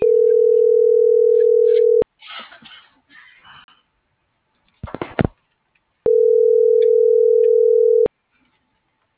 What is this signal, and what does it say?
Ringback tone (North America), 440 + 480 Hz.